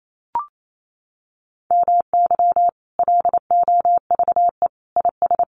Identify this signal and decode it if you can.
Morse code, heard as a keyed tone, about 28 wpm, 700 Hz. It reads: MY LOVE IS
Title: Sentences from Top 1500 Words 28wpm